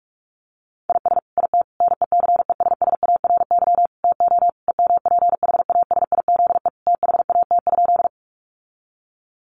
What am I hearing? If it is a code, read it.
Morse code, heard as a keyed tone, about 44 wpm, 700 Hz. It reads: IS IT NECESSARY TO EMPHASIZE THAT?
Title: Sentences from Top 1500 Words 45wpm